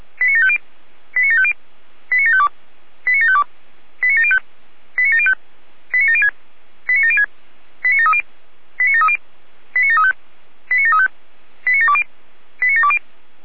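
Five-tone selective call (ZVEI), heard as quick runs of five tones, tones of about 70 ms. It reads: 89750 89750 89752 89752 89705 89705 89706 89706 89730 89730 89735 89735 89720 89720
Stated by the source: ZVEI five-tone call received on an SDR.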